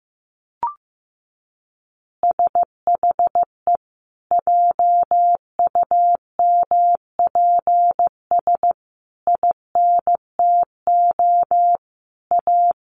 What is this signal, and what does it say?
Morse code, heard as a keyed tone, about 15 wpm, 700 Hz. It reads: SHE JUMPS INTO A
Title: Easy Phrases 15wpm